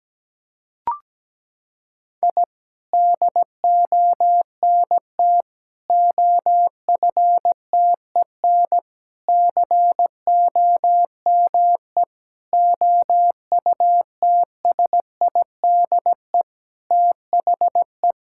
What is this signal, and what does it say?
Morse code, heard as a keyed tone, about 17 wpm, 700 Hz. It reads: I DONT OFTEN COME OUTSIDE THE
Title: Sentences from Top 1000 Words - Review Part 6 - 17wpm